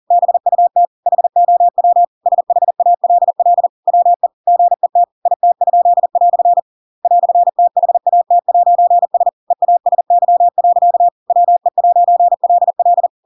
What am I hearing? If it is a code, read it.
Morse code, heard as a keyed tone, about 40 wpm, 700 Hz. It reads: BUT HOW SHALL WE GET IT?" "THAT'S EASY. WE'LL
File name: Pinocchio / Pinocchio ch8 40 40 ik1hge